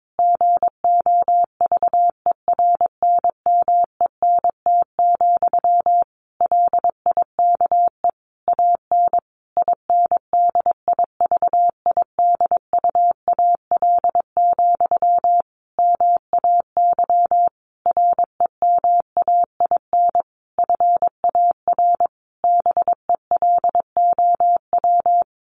Morse code, heard as a keyed tone, about 22 wpm, 700 Hz. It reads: GOVERNMENT, LIKE AN INDIVIDUAL, MAY REMAIN FAR BELOW
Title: Sentences from Top 1400 Words 22wpm